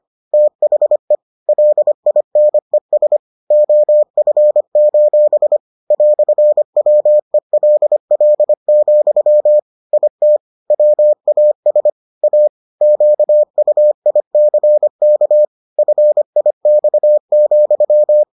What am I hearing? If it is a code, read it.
Morse code, heard as a keyed tone, about 25 wpm, 600 Hz. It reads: THE LINES OF: "WELL, IT WAS A QUICK FIX,